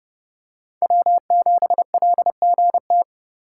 Morse code, heard as a keyed tone, about 31 wpm, 700 Hz. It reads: W7LGT